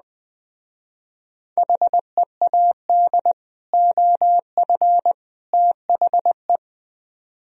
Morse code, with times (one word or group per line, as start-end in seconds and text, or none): 1.6-3.3 HEAD
3.7-5.1 OF
5.5-6.6 THE